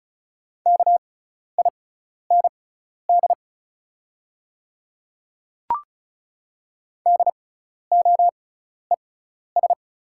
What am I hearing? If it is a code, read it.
Morse code, heard as a keyed tone, about 36 wpm, spaced as for 12 wpm, 700 Hz. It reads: KIND DOES
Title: 6x Character Spacing - Top 200 Words - 35wpm